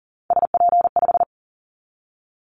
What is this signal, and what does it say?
Morse code, heard as a keyed tone, about 40 wpm, 700 Hz. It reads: SP5
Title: Callsign Prefixes for Top 25 DXCC - 40wpm